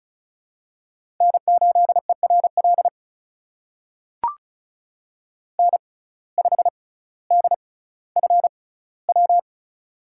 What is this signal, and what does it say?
Morse code, heard as a keyed tone, about 36 wpm, spaced as for 18 wpm, 700 Hz. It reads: N8ERL N5DFW
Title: USA Call Signs - 5 Characters - 6x Character Spacing - 35wpm